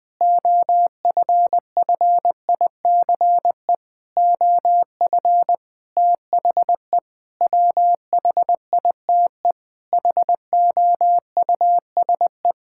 Morse code, heard as a keyed tone, about 21 wpm, 700 Hz. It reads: OFFICE OF THE WHITE HOUSE
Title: Sets of 8 Words - 20wpm